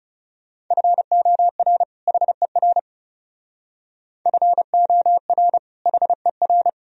Morse code, heard as a keyed tone, about 35 wpm, 700 Hz. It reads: FOR HER FOR HER
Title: Speed-Racing - Sets of 2 Words 30wpm